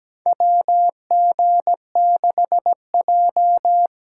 Morse code, heard as a keyed tone, about 17 wpm, 700 Hz. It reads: WG6J
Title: USA Call Signs - 4x Character Spacing - 17wpm